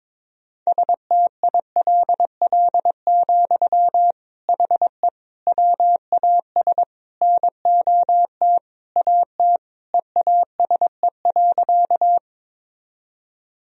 Morse code, heard as a keyed tone, about 23 wpm, 700 Hz. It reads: STILL, HE WAS NOT AT EASE.
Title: Sentences from Top 1000 Words - Review Part 1 - 22wpm